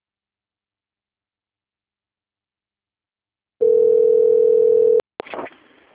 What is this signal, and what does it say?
Ringback tone (North America), 440 + 480 Hz.
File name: out- - - - - .4748